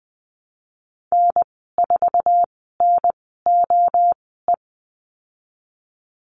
Morse code, heard as a keyed tone, about 20 wpm, spaced as for 15 wpm, 700 Hz. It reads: N4NOE
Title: 048 Lesson 9g - Number 4 - Call Signs 20wpm